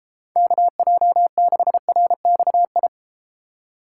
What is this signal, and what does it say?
Morse code, heard as a keyed tone, about 32 wpm, 700 Hz. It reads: KJ6RXI